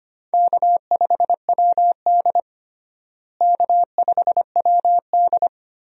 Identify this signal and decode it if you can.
Morse code, heard as a keyed tone, about 25 wpm, 700 Hz. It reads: K5WD K5WD
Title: USA Call Signs - 4 Characters - Repeated 3x - 25wpm